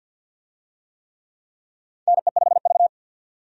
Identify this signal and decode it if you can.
Morse code, heard as a keyed tone, about 52 wpm, 700 Hz. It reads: NE5V
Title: USA Call Signs - 4 Characters - 50wpm